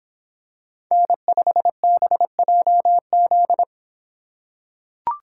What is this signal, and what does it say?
Morse code, heard as a keyed tone, about 26 wpm, 700 Hz. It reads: N5BJZ